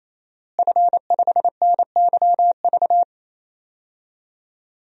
Morse code, heard as a keyed tone, about 28 wpm, 700 Hz. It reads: F5NYV
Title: DXCC-100 Call Signs - 5 Characters - Repeated 3x - 28wpm